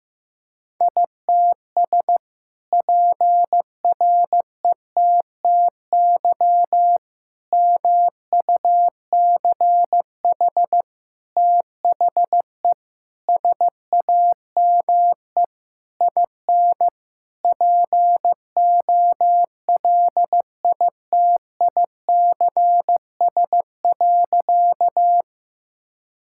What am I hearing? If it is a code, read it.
Morse code, heard as a keyed tone, about 15 wpm, 700 Hz. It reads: ITS PRETTY MUCH THE SAME IN POLITICS.
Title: Sentences from Top 1000 Words - Review Part 6 - 15wpm